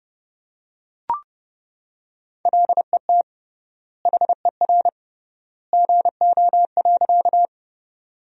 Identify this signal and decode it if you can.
Morse code, heard as a keyed tone, about 30 wpm, 700 Hz. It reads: LET HER GO.
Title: 3x Word-Spacing - Sentences from Top 300 Words - 30wpm